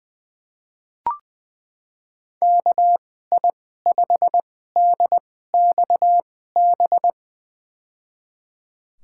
Morse code, heard as a keyed tone, about 20 wpm, spaced as for 14 wpm, 700 Hz. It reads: KI5DXB
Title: USA Call Signs - 6 Characters - 2x Character Spacing - 20wpm